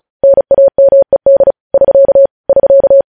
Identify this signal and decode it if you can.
Morse code, heard as a keyed tone, about 36 wpm, 550 Hz. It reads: NAMED <SK> <SK>
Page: news headlines in morse code at 35 wpm